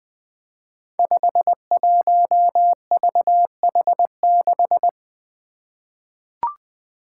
Morse code, heard as a keyed tone, about 20 wpm, 700 Hz. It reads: 51VH6
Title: Five Challenging Letter-Number Combinations 20wpm